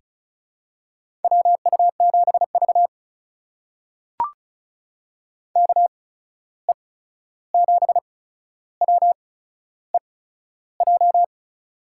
Morse code, heard as a keyed tone, about 36 wpm, spaced as for 13 wpm, 700 Hz. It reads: WU7V KE7WEJ
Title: USA Call Signs - 8x Character Spacing - 35wpm